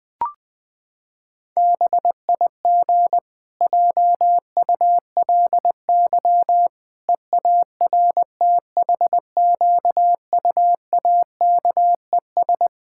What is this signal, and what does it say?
Morse code, heard as a keyed tone, about 20 wpm, 700 Hz. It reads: BIG JULY EARTHQUAKES
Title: Pangram Challenge v2 - 20wpm